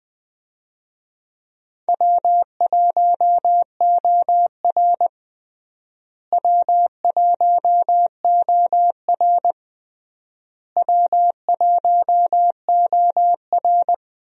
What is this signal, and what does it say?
Morse code, heard as a keyed tone, about 21 wpm, 700 Hz. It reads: W1OR W1OR W1OR
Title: USA Call Signs - 4 Characters - Repeated 3x - 20wpm